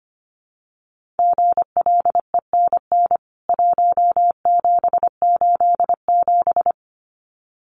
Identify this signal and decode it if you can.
Morse code, heard as a keyed tone, about 25 wpm, 700 Hz. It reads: GLENN 1787
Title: CWT Exchange 25wpm